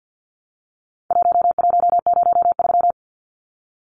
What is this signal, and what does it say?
Morse code, heard as a keyed tone, about 52 wpm, 700 Hz. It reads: 1103
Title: Numbers - Contest Style - 4-Digit 50wpm